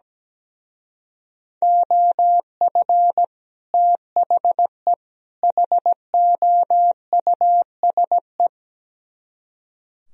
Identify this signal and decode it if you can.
Morse code, heard as a keyed tone, about 17 wpm, 700 Hz. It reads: OF THE HOUSE
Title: Speed-Racing - Sets of 3 Words 17wpm